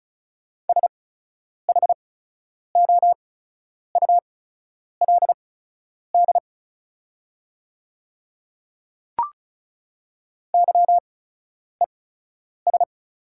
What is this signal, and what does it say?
Morse code, heard as a keyed tone, about 34 wpm, spaced as for 10 wpm, 690 Hz. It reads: SHOULD YES
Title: 8x Character Spacing - Top 500 Words - 35wpm